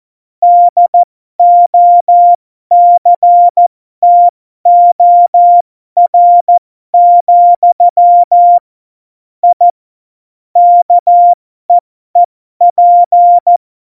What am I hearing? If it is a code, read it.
Morse code, heard as a keyed tone, about 14 wpm, spaced as for 12 wpm, 700 Hz. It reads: DOCTOR, I KEEP
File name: Morsekurs / Witze